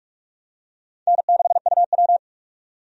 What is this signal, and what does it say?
Morse code, heard as a keyed tone, about 41 wpm, 700 Hz. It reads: N6UW
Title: USA Call Signs - 4 Characters - 45wpm